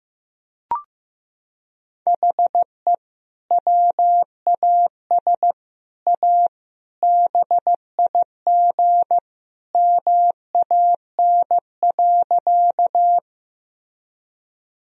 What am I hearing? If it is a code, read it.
Morse code, heard as a keyed tone, about 15 wpm, 700 Hz. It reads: HE WAS A BIG MAN.
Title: Sentences from Top 1000 Words - Review Part 1 - 15wpm